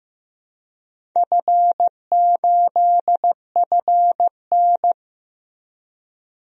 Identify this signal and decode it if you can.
Morse code, heard as a keyed tone, about 15 wpm, 700 Hz. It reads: F8FN
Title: DXCC-100 Call Signs - 4 Characters - Repeated 2x - 15wpm